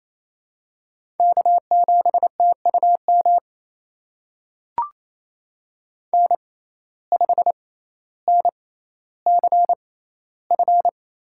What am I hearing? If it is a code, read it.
Morse code, heard as a keyed tone, about 28 wpm, spaced as for 14 wpm, 700 Hz. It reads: K7TUM N5NCF
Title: USA Call Signs - 5 Characters - 6x Character Spacing - 28wpm